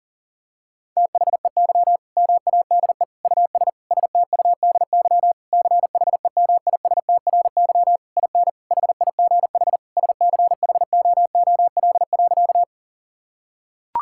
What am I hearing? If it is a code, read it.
Morse code, heard as a keyed tone, about 37 wpm, 700 Hz. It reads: THEY MADE US STUDY CHEMISTRY IN HIGH SCHOOL.
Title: Sentences from Top 1000 Words - Review Part 6 - 40wpm